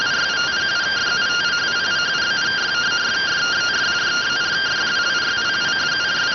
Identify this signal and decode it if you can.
Two-tone teleprinter signal, about 170 Hz shift.